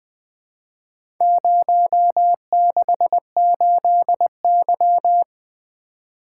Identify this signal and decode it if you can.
Morse code, heard as a keyed tone, about 21 wpm, 700 Hz. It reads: 068Y
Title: Four Letter-Number Combinations - 20wpm